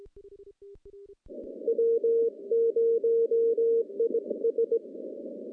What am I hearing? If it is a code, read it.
Morse code, heard as a keyed tone, about 20 wpm, 460 Hz. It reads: W0IS
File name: contests / CW-SS-k5tr- -0200